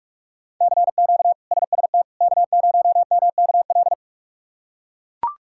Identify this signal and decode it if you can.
Morse code, heard as a keyed tone, about 42 wpm, 700 Hz. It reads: CQ SST K0MKL
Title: K1USN SST QSO Practice - 45wpm